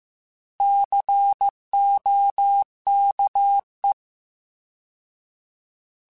Morse code, heard as a keyed tone, about 15 wpm, 800 Hz. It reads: COKE
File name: Alphabetical / Alphabetical mp3s / Cs @13